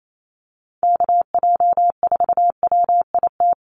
Morse code, heard as a keyed tone, about 28 wpm, 700 Hz. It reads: KJ4WIT
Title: USA Call Signs Repeated 3x 28wpm